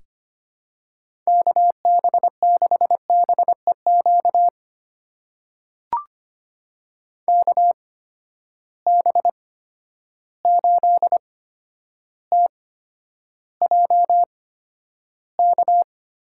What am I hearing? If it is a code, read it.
Morse code, heard as a keyed tone, about 25 wpm, spaced as for 11 wpm, 700 Hz. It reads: KB6BEQ KB8TJK
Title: USA Call Signs - 6 Characters - 8x Character Spacing - 25wpm